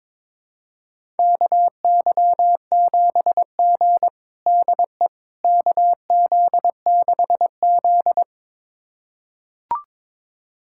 Morse code, heard as a keyed tone, about 22 wpm, 700 Hz. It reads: KY7G DE KZ6Z
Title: USA Callsign DE Callsign - 4 Characters - 22wpm